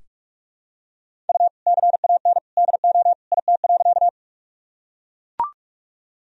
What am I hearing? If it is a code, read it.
Morse code, heard as a keyed tone, about 42 wpm, 700 Hz. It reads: U CAN DO IT.
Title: Sentences from Top 100 Words and Abbreviations 45wpm